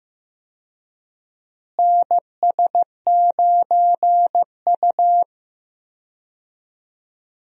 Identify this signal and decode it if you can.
Morse code, heard as a keyed tone, about 15 wpm, 700 Hz. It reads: NS9U
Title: USA Call Signs - 4 Characters - Repeated 2x - 15wpm